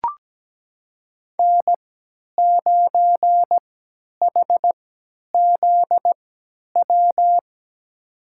Morse code, heard as a keyed tone, about 17 wpm, spaced as for 10 wpm, 700 Hz. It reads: N9HZW
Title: USA Call Signs - 5 Characters - 3x Character Spacing - 17wpm